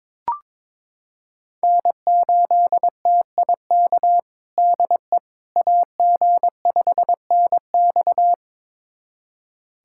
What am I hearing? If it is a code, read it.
Morse code, heard as a keyed tone, about 22 wpm, 700 Hz. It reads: N8TIK DE AG5NX
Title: USA Callsign DE Callsign - 5 Characters - 22wpm